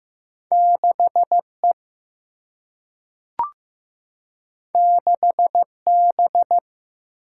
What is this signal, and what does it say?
Morse code, heard as a keyed tone, about 15 wpm, 700 Hz. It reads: 6E 6B
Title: Single-Letter Single-Number 15wpm